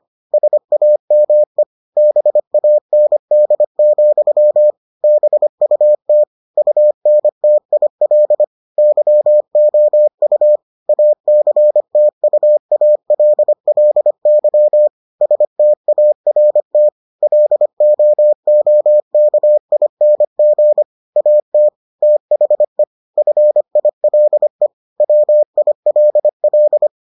Morse code, heard as a keyed tone, about 25 wpm, 600 Hz. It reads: SAME BAND, BUT UNTIL YOU ACTUALLY START LOOKING AT THE FILE WILL